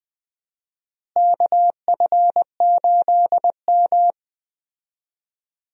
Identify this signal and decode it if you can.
Morse code, heard as a keyed tone, about 20 wpm, 700 Hz. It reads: KF8M
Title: USA Call Signs - 4 Characters - Repeated 2x - 20wpm